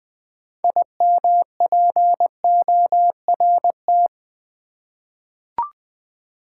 Morse code, heard as a keyed tone, about 20 wpm, 700 Hz. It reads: IMPORT
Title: 6 Letter Words - First 3 Letters Given - 20wpm